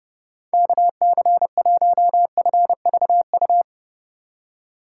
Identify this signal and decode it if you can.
Morse code, heard as a keyed tone, about 28 wpm, 700 Hz. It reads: KC1FVU